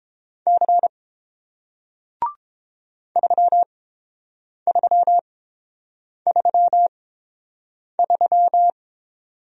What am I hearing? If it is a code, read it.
Morse code, heard as a keyed tone, about 30 wpm, spaced as for 7 wpm, 700 Hz. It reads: C 3333